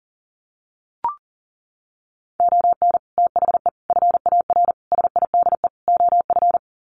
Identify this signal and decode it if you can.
Morse code, heard as a keyed tone, about 40 wpm, 700 Hz. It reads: ON THE FAR SIDE OF
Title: Sets of 5 Words 40wpm